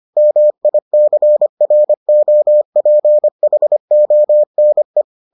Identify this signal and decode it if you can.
Morse code, heard as a keyed tone, about 25 wpm, 600 Hz. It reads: MICROPHONE